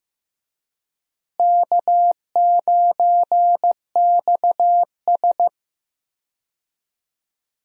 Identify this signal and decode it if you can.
Morse code, heard as a keyed tone, about 15 wpm, 700 Hz. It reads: K9XS
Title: USA Call Signs - 4 Characters - Repeated 3x - 15wpm